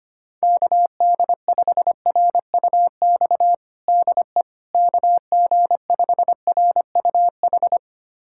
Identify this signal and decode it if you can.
Morse code, heard as a keyed tone, about 25 wpm, 700 Hz. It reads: KD5RUX DE KG5RUH
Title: USA Callsign DE Callsign - 6 Characters - 25wpm